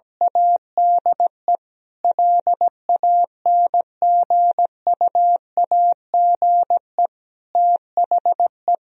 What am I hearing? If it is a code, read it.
Morse code, heard as a keyed tone, about 17 wpm, 700 Hz. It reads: ADE LANGUAGE THE
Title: Sentences from Top 1000 Words - Review Part 3 - 17wpm